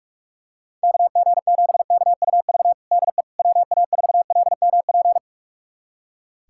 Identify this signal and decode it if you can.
Morse code, heard as a keyed tone, about 42 wpm, 700 Hz. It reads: KG7KUV DE WA4LMP
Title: USA Callsign DE Callsign - 6 Characters - 45wpm